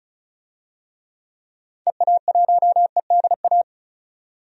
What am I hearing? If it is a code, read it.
Morse code, heard as a keyed tone, about 36 wpm, 700 Hz. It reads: EA1EDA
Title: DXCC-100 Call Signs - 6 Characters - 35wpm